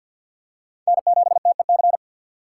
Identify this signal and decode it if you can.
Morse code, heard as a keyed tone, about 51 wpm, 700 Hz. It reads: N7TEC
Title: USA Call Signs - 5 Characters - 3x Character Spacing - 50wpm